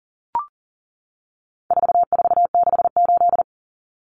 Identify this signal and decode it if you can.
Morse code, heard as a keyed tone, about 40 wpm, 700 Hz. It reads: 4468
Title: Numbers - 4-Digit 40wpm